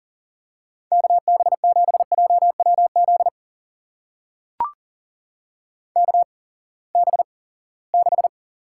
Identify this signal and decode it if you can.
Morse code, heard as a keyed tone, about 40 wpm, spaced as for 23 wpm, 700 Hz. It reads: KB7JWZ KB6
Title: USA Call Signs - 6 Characters - 8x Character Spacing - 40wpm